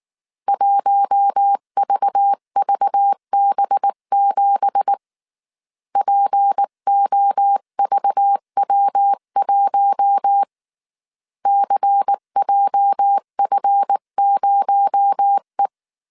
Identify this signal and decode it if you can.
Morse code, heard as a keyed tone, about 19 wpm, 800 Hz. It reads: 1VVB7 POVW1 CJF0E
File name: CW / DET / 1 9 1500C S C 18wpm 2539